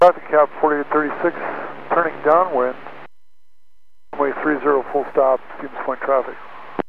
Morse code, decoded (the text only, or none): none